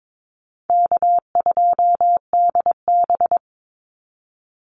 Morse code, heard as a keyed tone, about 22 wpm, 700 Hz. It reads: K2DB
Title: Easy Call Signs - 22wpm